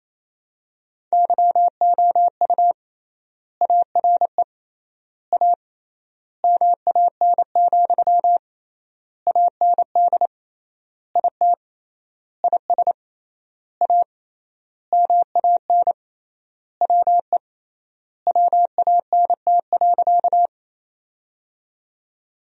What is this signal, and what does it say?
Morse code, heard as a keyed tone, about 28 wpm, 700 Hz. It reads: YOU ARE A MAN, AND IT IS A MAN WE WANT.
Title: 3x Word-Spacing - Sentences from Top 200 Words - 28wpm